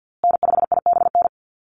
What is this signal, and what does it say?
Morse code, heard as a keyed tone, about 50 wpm, 700 Hz. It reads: N5IBN
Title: USA Call Signs Repeated 2x 50wpm